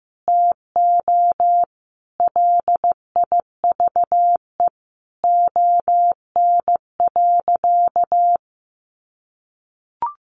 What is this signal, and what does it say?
Morse code, heard as a keyed tone, about 15 wpm, 700 Hz. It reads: TO LIVE ON.
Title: Sentences from Top 800 Words 15wpm